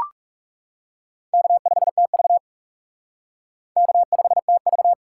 Morse code, heard as a keyed tone, about 37 wpm, 700 Hz. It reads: K5TV K5TV
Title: Speed Racing - USA Call Signs 30wpm